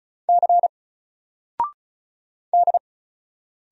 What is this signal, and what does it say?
Morse code, heard as a keyed tone, about 35 wpm, 700 Hz. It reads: C D
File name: Single-Letter-Number-35wpm